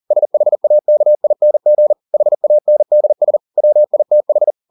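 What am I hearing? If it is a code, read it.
Morse code, heard as a keyed tone, about 40 wpm, 600 Hz. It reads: SHAKING HANDS WITH
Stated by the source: Morse code at 40 wpm, practice text, CW med Gnister.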